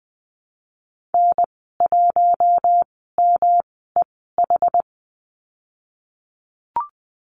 Morse code, heard as a keyed tone, about 20 wpm, spaced as for 14 wpm, 700 Hz. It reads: N1MEH
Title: 140 Lesson 23g - Letter f - Call Signs 20wpm